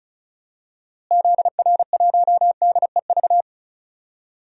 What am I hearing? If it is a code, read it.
Morse code, heard as a keyed tone, about 36 wpm, 700 Hz. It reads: ZR1DEV
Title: DXCC-100 Call Signs - 6 Characters - Repeated 3x - 35wpm